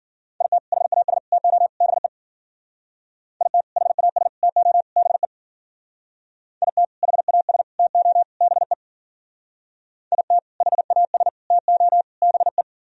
Morse code, heard as a keyed tone, about 52 wpm, 700 Hz. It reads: IT HAS TO BE IT HAS TO BE IT HAS TO BE IT HAS TO BE